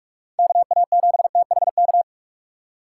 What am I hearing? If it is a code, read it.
Morse code, heard as a keyed tone, about 42 wpm, 700 Hz. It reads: KA7THK